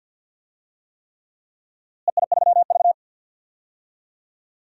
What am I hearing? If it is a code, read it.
Morse code, heard as a keyed tone, about 49 wpm, 700 Hz. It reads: EI3V
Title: DXCC-100 Call Signs - 4 Characters - Repeated 3x - 50wpm